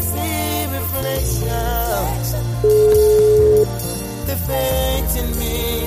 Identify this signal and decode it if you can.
Ringback tone (Europe, 425 Hz).